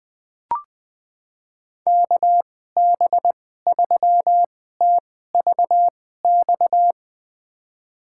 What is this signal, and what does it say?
Morse code, heard as a keyed tone, about 20 wpm, spaced as for 14 wpm, 700 Hz. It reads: KB3TVX
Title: USA Call Signs - 6 Characters - 2x Character Spacing - 20wpm